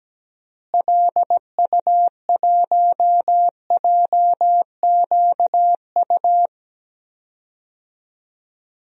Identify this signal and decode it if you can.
Morse code, heard as a keyed tone, about 17 wpm, 700 Hz. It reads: LU1JQU
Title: DXCC-100 Call Signs - 6 Characters - Repeated 3x - 17wpm